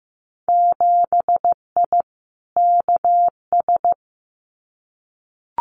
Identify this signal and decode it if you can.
Morse code, heard as a keyed tone, about 15 wpm, 700 Hz. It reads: 7I KS
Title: Winter Field Day Exchanges Encore 15wpm